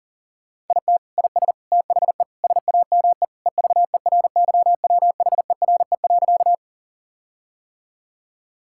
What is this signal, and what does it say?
Morse code, heard as a keyed tone, about 40 wpm, 700 Hz. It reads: IT IS THE SAME EVERYWHERE.
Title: Sentences from Top 1600 Words 40wpm